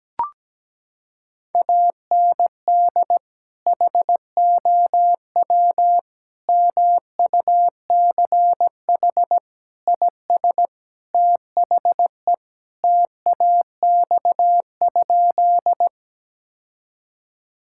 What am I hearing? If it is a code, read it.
Morse code, heard as a keyed tone, about 17 wpm, 700 Hz. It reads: AND HOW MUCH IS THE TAX?
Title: Sentences from Top 1000 Words - Review Part 5 - 17wpm